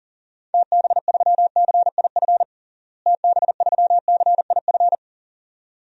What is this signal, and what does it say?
Morse code, heard as a keyed tone, about 40 wpm, 700 Hz. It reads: TB3CIF TB3CIF
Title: DXCC-100 Call Signs - 6 Characters - Repeated 2x - 40wpm